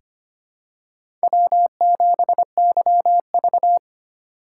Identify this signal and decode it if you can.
Morse code, heard as a keyed tone, about 25 wpm, 700 Hz. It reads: W7YV
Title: USA Call Signs - 4 Characters - 3x Character Spacing - 25wpm